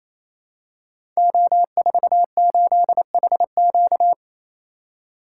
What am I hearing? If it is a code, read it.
Morse code, heard as a keyed tone, about 27 wpm, 700 Hz. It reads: O48HQ
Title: Code Groups - 5 Letter:Numbers - 28wpm